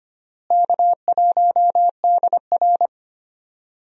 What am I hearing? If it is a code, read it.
Morse code, heard as a keyed tone, about 25 wpm, 700 Hz. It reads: K1DR
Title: USA Call Signs - 4 Characters - Repeated 2x - 25wpm